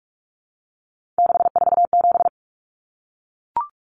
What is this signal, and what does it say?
Morse code, heard as a keyed tone, about 44 wpm, 700 Hz. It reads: 647
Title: Numbers - 3-Digit 45wpm